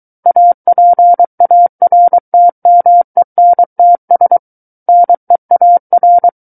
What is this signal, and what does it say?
Morse code, heard as a keyed tone, about 23 wpm, 700 Hz. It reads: APARTMENTS NEAR
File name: Book One / 14-Chapter 14 - 23 WPM